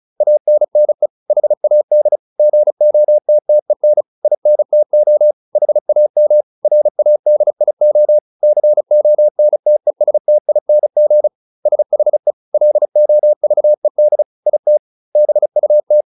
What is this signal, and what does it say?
Morse code, heard as a keyed tone, about 36 wpm, 600 Hz. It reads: ANNE HAD GOTTEN INTO HAM RADIO CONTESTING SHE LOVED IT BUT
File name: SS303 35